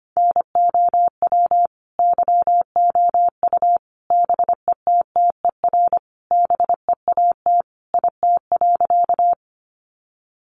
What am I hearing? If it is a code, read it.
Morse code, heard as a keyed tone, about 25 wpm, 700 Hz. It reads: NOW YOU BETTER BEAT IT.
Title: Sentences from Top 1300 Words 25wpm